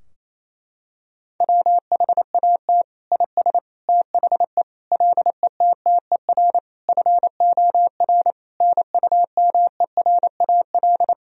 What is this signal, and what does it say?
Morse code, heard as a keyed tone, about 27 wpm, 700 Hz. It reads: WHAT IS THE LETTER FOR NUMERAL
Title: Sentences from Top 1000 Words - Review Part 2 - 28wpm